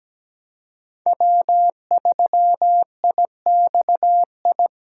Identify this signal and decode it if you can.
Morse code, heard as a keyed tone, about 17 wpm, 700 Hz. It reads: W3IXI
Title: USA Call Signs - 5 Characters - 2x Character Spacing - 17wpm